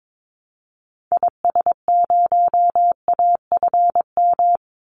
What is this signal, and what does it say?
Morse code, heard as a keyed tone, about 22 wpm, 700 Hz. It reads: IS0AFM
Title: Call Signs - 22wpm